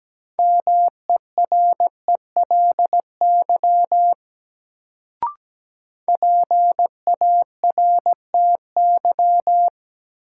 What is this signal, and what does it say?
Morse code, heard as a keyed tone, about 17 wpm, 700 Hz. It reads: MERELY PARTY
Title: Sentences from Top 1400 Words - New Word Drill - 17wpm